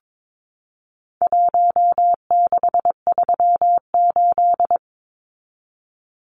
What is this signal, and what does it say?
Morse code, heard as a keyed tone, about 22 wpm, 700 Hz. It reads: 1638
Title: Numbers - Contest Style - 4-Digit 22wpm